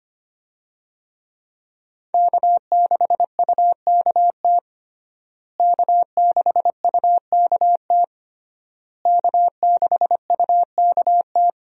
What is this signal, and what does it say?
Morse code, heard as a keyed tone, about 25 wpm, 700 Hz. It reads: K6UKT K6UKT K6UKT
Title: USA Call Signs - 5 Characters - Repeated 3x - 25wpm